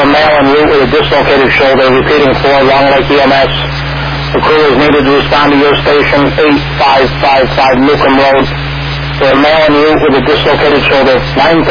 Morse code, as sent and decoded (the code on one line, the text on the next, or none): none